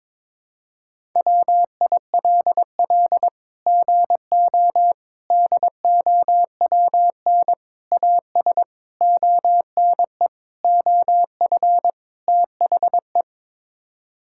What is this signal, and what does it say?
Morse code, heard as a keyed tone, about 22 wpm, 700 Hz. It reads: WILL GO DOWN AS ONE OF THE
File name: practice / 22wpm / Sets of 7 Words - 22wpm